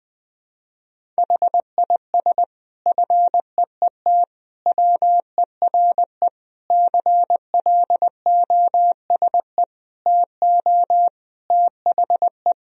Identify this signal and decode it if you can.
Morse code, heard as a keyed tone, about 21 wpm, 700 Hz. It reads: HIS FEET WERE CLOSE TO THE
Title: Sentences from Top 1000 Words - Review Part 1 - 20wpm